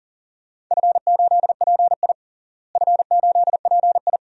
Morse code, heard as a keyed tone, about 40 wpm, 700 Hz. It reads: F8PI F8PI
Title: DXCC-100 Call Signs - 4 Characters - Repeated 2x - 40wpm